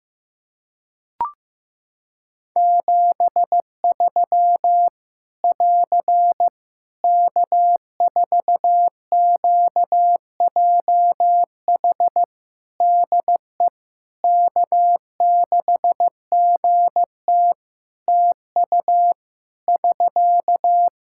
Morse code, heard as a keyed tone, about 15 wpm, 700 Hz. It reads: 73 <AR> K4QJH DE K6GT TU <SK>
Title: Practice QSO - Exchange 4 - Ending - 15wpm